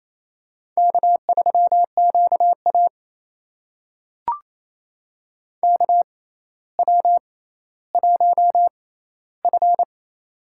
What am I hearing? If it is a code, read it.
Morse code, heard as a keyed tone, about 28 wpm, spaced as for 14 wpm, 700 Hz. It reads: K3QA KW1F
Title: USA Call Signs - 4 Characters - 6x Character Spacing - 28wpm